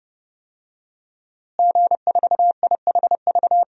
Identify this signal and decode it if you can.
Morse code, heard as a keyed tone, about 32 wpm, 700 Hz. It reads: G4IHV
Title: DXCC-100 Call Signs - 5 Characters - Repeated 2x - 30wpm